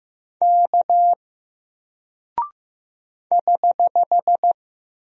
Morse code, heard as a keyed tone, about 15 wpm, 700 Hz. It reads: K <HH>
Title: Single Character (Letter:Number:Punctuation:Prosign) Alt 15wpm